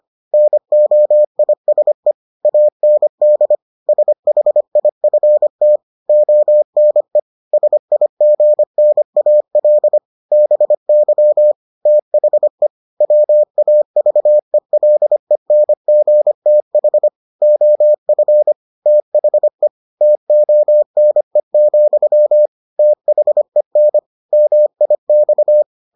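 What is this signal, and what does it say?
Morse code, heard as a keyed tone, about 25 wpm, 600 Hz. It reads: NOISE AND SHIFT ONE SIGNAL BY THE WAVELENGTH OF THE TONE, THEN MIX
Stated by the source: Morse edition of the Foundations of Amateur Radio episodes.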